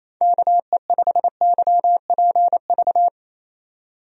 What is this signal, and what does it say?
Morse code, heard as a keyed tone, about 28 wpm, 700 Hz. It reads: KE5YPV
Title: USA Call Signs - 6 Characters - 28wpm